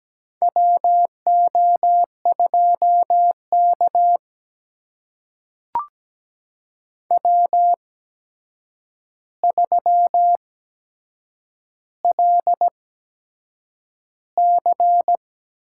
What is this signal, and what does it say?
Morse code, heard as a keyed tone, about 17 wpm, spaced as for 7 wpm, 700 Hz. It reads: WO2K W3LC
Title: USA Call Signs - 4 Characters - 8x Character Spacing - 17wpm